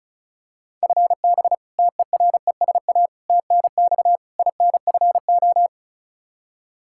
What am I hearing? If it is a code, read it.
Morse code, heard as a keyed tone, about 36 wpm, 700 Hz. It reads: FB TERESA TNX INFO
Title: Practice QSO - Exchange 3 - More Info - 35wpm